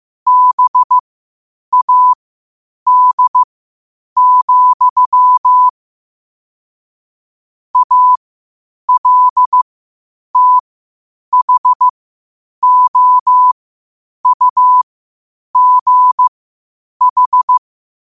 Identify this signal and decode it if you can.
Morse code, heard as a keyed tone, about 15 wpm, spaced as for 8 wpm, 1000 Hz. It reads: BAD, ALTHOUGH
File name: qotdfiles / 8wpm